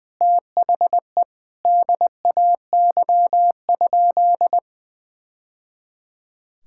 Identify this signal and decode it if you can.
Morse code, heard as a keyed tone, about 21 wpm, 700 Hz. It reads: THE DAY?
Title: Sentences from Top 1000 Words - Review Part 5 - 20wpm